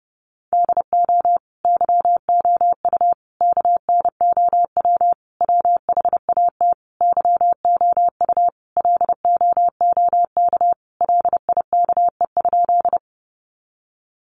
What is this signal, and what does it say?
Morse code, heard as a keyed tone, about 30 wpm, 700 Hz. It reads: DO YOU KNOW WHAT YOU LOOK LIKE?
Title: Sentences from Top 100 Words 30wpm